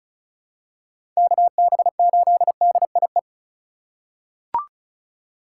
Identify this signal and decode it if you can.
Morse code, heard as a keyed tone, about 36 wpm, 700 Hz. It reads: KB8DIE
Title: USA Call Signs - 6 Characters - 2x Character Spacing - 35wpm